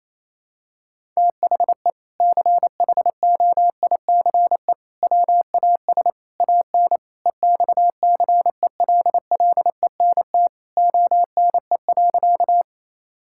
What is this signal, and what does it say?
Morse code, heard as a keyed tone, about 28 wpm, 700 Hz. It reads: THE CHOICE WAS AN EXCELLENT ONE.
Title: Sentences from Top 1000 Words - Review Part 6 - 28wpm